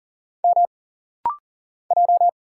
Morse code, heard as a keyed tone, about 40 wpm, 700 Hz. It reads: M J
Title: Single Letter-Number - Rapid-Fire 40wpm